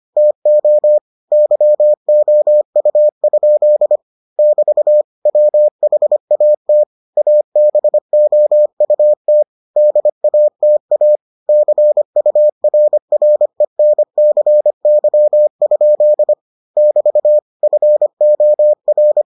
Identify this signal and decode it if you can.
Morse code, heard as a keyed tone, about 25 wpm, 600 Hz. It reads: TO YOU? <BT> WHAT ABOUT DATA CURRENCY? <BT> FOR